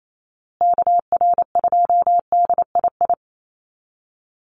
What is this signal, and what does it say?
Morse code, heard as a keyed tone, about 28 wpm, 700 Hz. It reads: KR2DII
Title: USA Call Signs Repeated 2x 28wpm